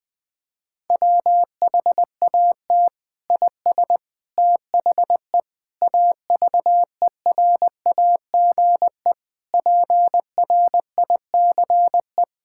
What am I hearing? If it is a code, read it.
Morse code, heard as a keyed tone, about 21 wpm, 700 Hz. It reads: WHAT IS THE AVERAGE PRICE
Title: Sentences from Top 1000 Words - Review Part 6 - 20wpm